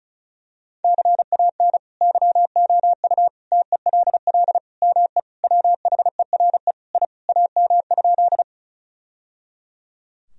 Morse code, heard as a keyed tone, about 34 wpm, 700 Hz. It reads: CAN YOU TELL ME WHERE I AM?
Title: Sentences from Top 1000 Words - Review Part 2 - 35wpm